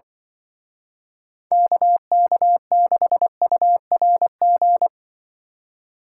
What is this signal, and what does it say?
Morse code, heard as a keyed tone, about 24 wpm, 700 Hz. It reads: KK6URG
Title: Speed Racing - USA 2x3 Call Signs 20wpm